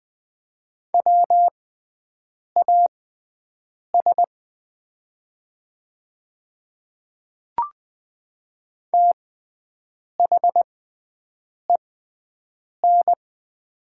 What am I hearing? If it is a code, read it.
Morse code, heard as a keyed tone, about 20 wpm, spaced as for 7 wpm, 700 Hz. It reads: WAS THEN
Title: 6x Character Spacing - Top 100 Words - 20wpm